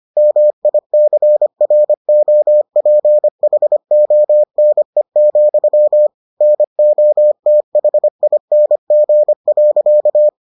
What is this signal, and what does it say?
Morse code, heard as a keyed tone, about 25 wpm, 600 Hz. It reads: MICROPHONE, NOTHING.